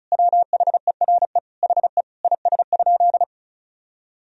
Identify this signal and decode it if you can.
Morse code, heard as a keyed tone, about 36 wpm, 700 Hz. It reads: WHERE HE IS?
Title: Sentences from Top 300 Words and Abbreviations 35wpm